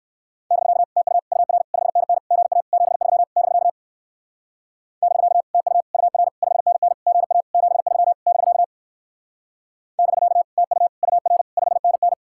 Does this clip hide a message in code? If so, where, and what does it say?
Yes: Morse code from 0.5 to 12.1 s, heard as a keyed tone, about 57 wpm, 700 Hz. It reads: <BK> TU UR 5NN GA 73 <BK> <BK> TU UR 5NN GA 73 <BK> <BK> TU UR 5NN